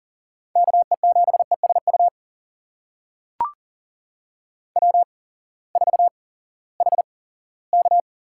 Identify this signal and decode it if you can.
Morse code, heard as a keyed tone, about 40 wpm, spaced as for 20 wpm, 700 Hz. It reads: KE7ESU W4HK
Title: USA Call Signs - 8x Character Spacing - 40wpm